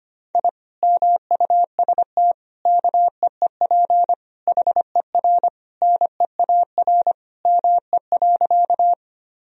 Morse code, heard as a keyed tone, about 25 wpm, 700 Hz. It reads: I MUST KEEP HER NEAR ME.
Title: Sentences from Top 1000 Words - Review Part 1 - 25wpm